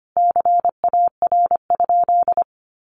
Morse code, heard as a keyed tone, about 25 wpm, 700 Hz. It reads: CAR?
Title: Sentences from Top 1200 Words 25wpm